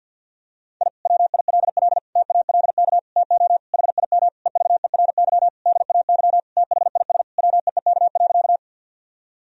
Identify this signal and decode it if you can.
Morse code, heard as a keyed tone, about 50 wpm, 700 Hz. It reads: I WILL TALK TO HIM EVERY DAY THIS WEEK.
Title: Sentences from Top 1000 Words - Review Part 2 - 50wpm